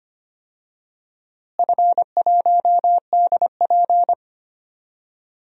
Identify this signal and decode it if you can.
Morse code, heard as a keyed tone, about 25 wpm, 700 Hz. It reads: F1DP
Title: DXCC-100 Call Signs - 4 Characters - Repeated 2x - 25wpm